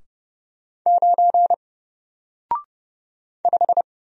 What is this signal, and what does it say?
Morse code, heard as a keyed tone, about 30 wpm, 700 Hz. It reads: 9 5
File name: Single-Letter-Number-30wpm